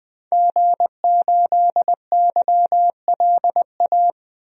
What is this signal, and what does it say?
Morse code, heard as a keyed tone, about 20 wpm, 700 Hz. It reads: G8YLA
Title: DXCC-100 Call Signs - 5 Characters - 20wpm